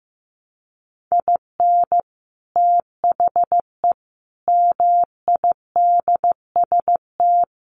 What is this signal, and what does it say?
Morse code, heard as a keyed tone, about 15 wpm, 700 Hz. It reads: IN THE MIDST
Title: Sets of 5 Words 15wpm